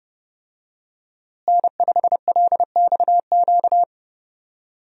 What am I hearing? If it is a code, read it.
Morse code, heard as a keyed tone, about 32 wpm, 700 Hz. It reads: N5LXQ
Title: USA Call Signs - 5 Characters - Repeated 2x - 30wpm